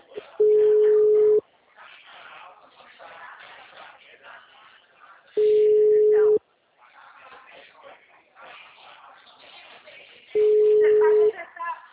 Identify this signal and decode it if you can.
Ringback tone (Europe, 425 Hz), with speech.